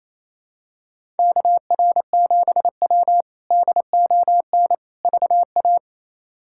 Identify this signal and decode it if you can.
Morse code, heard as a keyed tone, about 28 wpm, 700 Hz. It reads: KR7W DON VA
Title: K1USN SST QSO Practice - 28wpm